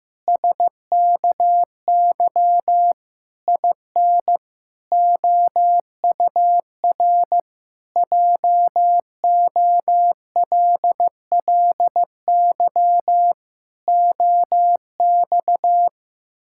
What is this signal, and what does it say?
Morse code, heard as a keyed tone, about 16 wpm, 700 Hz. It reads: SKY IN OUR JOLLY OX
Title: Pangram Challenge v2 - 15wpm